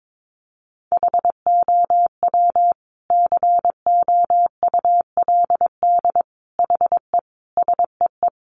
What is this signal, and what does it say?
Morse code, heard as a keyed tone, about 22 wpm, 700 Hz. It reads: HOW COULD HE SEE
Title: Sentences from Top 1300 Words 22wpm